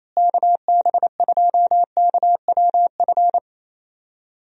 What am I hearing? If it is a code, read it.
Morse code, heard as a keyed tone, about 27 wpm, 700 Hz. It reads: KB2KWF
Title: USA Call Signs - 6 Characters - Repeated 2x - 28wpm